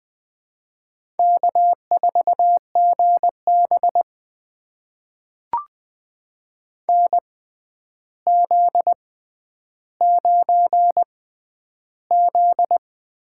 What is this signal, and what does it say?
Morse code, heard as a keyed tone, about 20 wpm, spaced as for 10 wpm, 700 Hz. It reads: K4GB NZ9Z
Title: USA Call Signs - 4 Characters - 6x Character Spacing - 20wpm